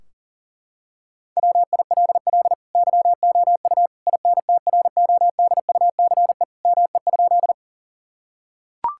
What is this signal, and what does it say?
Morse code, heard as a keyed tone, about 40 wpm, 700 Hz. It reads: WILL YOU INTRODUCE ME?
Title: Sentences from Top 1600 Words 40wpm